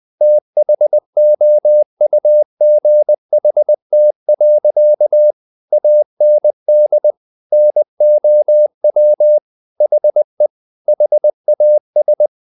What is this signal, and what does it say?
Morse code, heard as a keyed tone, about 20 wpm, 600 Hz. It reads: THOUGHT. AND NOW HE HAS
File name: the old man and the sea part 2 20wpm